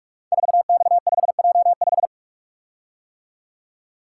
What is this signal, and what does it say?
Morse code, heard as a keyed tone, about 44 wpm, 700 Hz. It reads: 4X5J5
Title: Code Groups - 5 Letter:Numbers - 45wpm